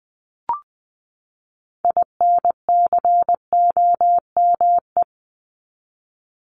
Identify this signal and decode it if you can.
Morse code, heard as a keyed tone, about 20 wpm, 700 Hz. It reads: INCOME
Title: Top 751-1000 Words 20wpm